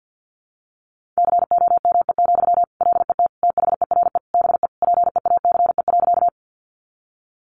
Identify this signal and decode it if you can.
Morse code, heard as a keyed tone, about 50 wpm, 700 Hz. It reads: COME, LET THERE BE PEACE.
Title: Sentences from Top 1500 Words 50wpm